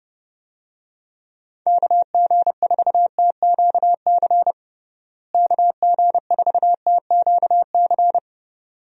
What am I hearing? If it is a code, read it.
Morse code, heard as a keyed tone, about 30 wpm, 700 Hz. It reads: KG4TQC KG4TQC
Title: USA Call Signs - 6 Characters - Repeated 2x - 30wpm